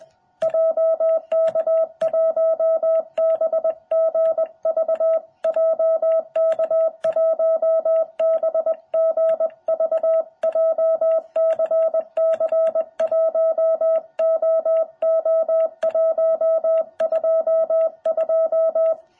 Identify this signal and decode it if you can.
Morse code, heard as a keyed tone, about 21 wpm, 660 Hz. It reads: JK1BGV JK1BGV JCC1OO122
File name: jh8cbh / 2021cw